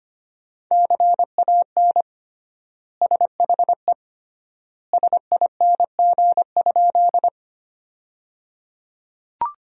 Morse code, heard as a keyed tone, about 25 wpm, 700 Hz. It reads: CAN SHE SING?
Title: 3x Word-Spacing - Sentences from Top 400 Words - 25wpm